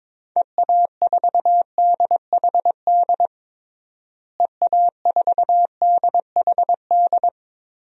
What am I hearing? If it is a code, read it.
Morse code, heard as a keyed tone, about 21 wpm, 700 Hz. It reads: EA4DHD EA4DHD
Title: DXCC-100 Call Signs - 6 Characters - Repeated 2x - 22wpm